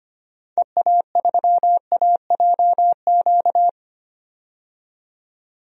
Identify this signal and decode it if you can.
Morse code, heard as a keyed tone, about 25 wpm, 700 Hz. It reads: EA3AJQ
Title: DXCC-100 Call Signs - 6 Characters - 25wpm